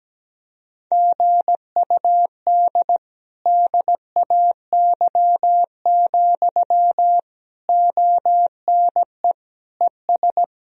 Morse code, heard as a keyed tone, about 17 wpm, 700 Hz. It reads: GUD DAY, ONE ES
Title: Sentences from Top 200 Words and Abbreviations 17wpm